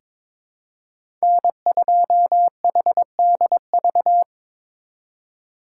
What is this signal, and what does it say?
Morse code, heard as a keyed tone, about 22 wpm, 700 Hz. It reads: N2HDV
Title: USA Call Signs - 4x Character Spacing - 22wpm